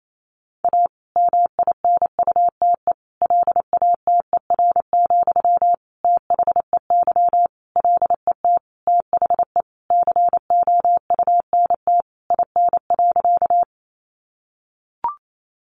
Morse code, heard as a keyed tone, about 28 wpm, 700 Hz. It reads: A MINUTE LATER, THEY LET THE COUNT IN.
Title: Sentences from Top 1500 Words 28wpm